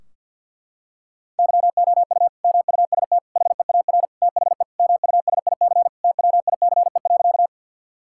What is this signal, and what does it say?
Morse code, heard as a keyed tone, about 50 wpm, 700 Hz. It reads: YOU MUST HEAR THE MUSIC TWICE.
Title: Sentences from Top 1600 Words 50wpm